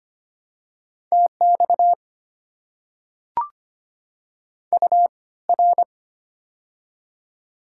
Morse code, heard as a keyed tone, about 25 wpm, spaced as for 18 wpm, 690 Hz. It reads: TX UR
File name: Top 90 QSO Elements - 3x Character Spacing - 25wpm